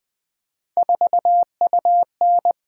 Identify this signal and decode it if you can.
Morse code, heard as a keyed tone, about 20 wpm, 700 Hz. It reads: 4UN
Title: Three Letter-Number Combinations - 20wpm